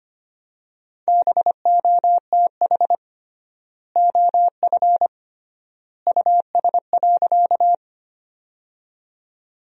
Morse code, heard as a keyed tone, about 25 wpm, 700 Hz. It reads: BOTH OF US.
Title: 3x Word-Spacing - Sentences from Top 300 Words - 25wpm